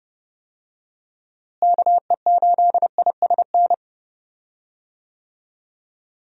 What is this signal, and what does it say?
Morse code, heard as a keyed tone, about 31 wpm, 700 Hz. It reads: KE8ISN
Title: USA Call Signs - 6 Characters - Repeated 2x - 30wpm